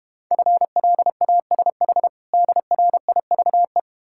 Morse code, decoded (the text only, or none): FLASH DRIVE